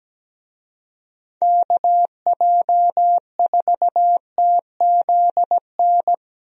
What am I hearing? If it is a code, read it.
Morse code, heard as a keyed tone, about 17 wpm, 700 Hz. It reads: KJ4TZN
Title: USA Call Signs - 6 Characters - Repeated 2x - 17wpm